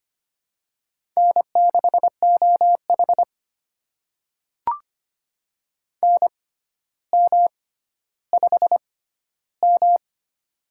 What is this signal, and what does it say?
Morse code, heard as a keyed tone, about 25 wpm, spaced as for 13 wpm, 700 Hz. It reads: N6OH NM5M
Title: USA Call Signs - 4 Characters - 6x Character Spacing - 25wpm